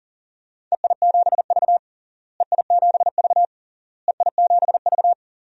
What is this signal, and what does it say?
Morse code, heard as a keyed tone, about 40 wpm, 700 Hz. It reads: EI7V EI7V EI7V
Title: DXCC-100 Call Signs - 4 Characters - Repeated 3x - 40wpm